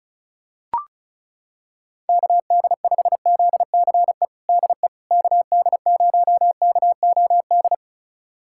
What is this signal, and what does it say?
Morse code, heard as a keyed tone, about 34 wpm, 700 Hz. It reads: KD5ZCE DE KD0KOD
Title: USA Callsign DE Callsign - 6 Characters - 35wpm